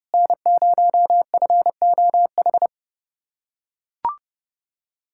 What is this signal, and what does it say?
Morse code, heard as a keyed tone, about 30 wpm, 700 Hz. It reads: N0FOH
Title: USA Callsign DE Callsign - 5 Characters - 30wpm